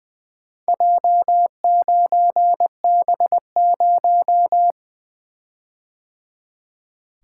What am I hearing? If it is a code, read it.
Morse code, heard as a keyed tone, about 20 wpm, 700 Hz. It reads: J9B0
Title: Four Letter-Number Combinations - 20wpm